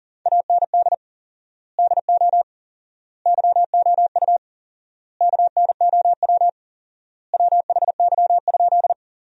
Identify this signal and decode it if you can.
Morse code, heard as a keyed tone, about 40 wpm, 700 Hz. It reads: AND DO YOU KNOW WHY?
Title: 4x Word-Spacing - Sentences from Top 200 Words - 40wpm